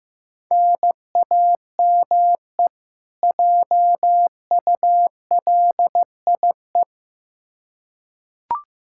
Morse code, heard as a keyed tone, about 15 wpm, 700 Hz. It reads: NAME JULIE
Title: K1USN SST QSO Practice 15wpm